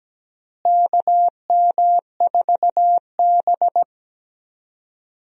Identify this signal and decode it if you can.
Morse code, heard as a keyed tone, about 17 wpm, 700 Hz. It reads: KM4B
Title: USA Call Signs - 4 Characters - Repeated 3x - 17wpm